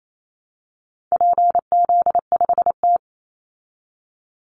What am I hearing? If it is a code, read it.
Morse code, heard as a keyed tone, about 28 wpm, 700 Hz. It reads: PZ5T
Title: Call Signs Repeated 2x - 28wpm